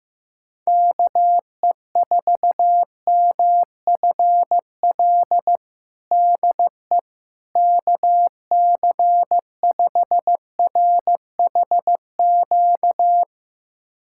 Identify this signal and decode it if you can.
Morse code, heard as a keyed tone, about 15 wpm, 700 Hz. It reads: KE4MFL DE KC5RHQ
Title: USA Callsign DE Callsign 15wpm